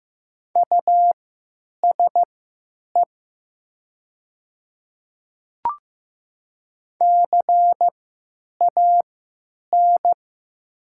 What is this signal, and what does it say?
Morse code, heard as a keyed tone, about 15 wpm, spaced as for 9 wpm, 700 Hz. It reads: USE CAN
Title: 3x Character Spacing - Top 100 Words - 15wpm